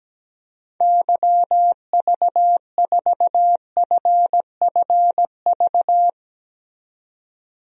Morse code, heard as a keyed tone, about 17 wpm, 700 Hz. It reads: YV4FFV
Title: DXCC-100 Call Signs - 6 Characters - 17wpm